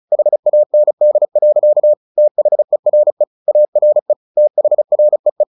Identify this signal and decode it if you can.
Morse code, heard as a keyed tone, about 34 wpm, 600 Hz. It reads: HAND. THERE ARE THREE
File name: the old man and the sea / the old man and the sea 1 35wpm